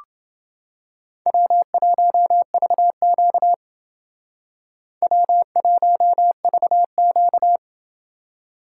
Morse code, heard as a keyed tone, about 28 wpm, 700 Hz. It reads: W1VQ W1VQ